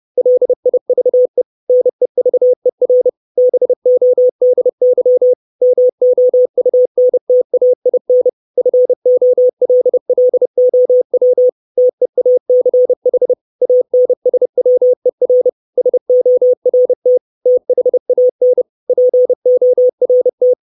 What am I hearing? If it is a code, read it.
Morse code, heard as a keyed tone, about 30 wpm, 500 Hz. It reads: LIVE NEVER BODY MOUNTAIN FOLLOW TEACH ANSWER SORT THAN PORT
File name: 632 Words 30WPM